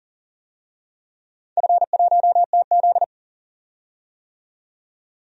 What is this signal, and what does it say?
Morse code, heard as a keyed tone, about 40 wpm, 700 Hz. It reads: F1TZ
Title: DXCC-100 Call Signs - 4 Characters - 40wpm